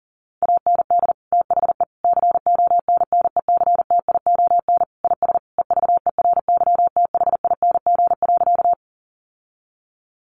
Morse code, heard as a keyed tone, about 40 wpm, 700 Hz. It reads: AND THE CONNECTION IS EVERYTHING.
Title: Sentences from Top 1300 Words 40wpm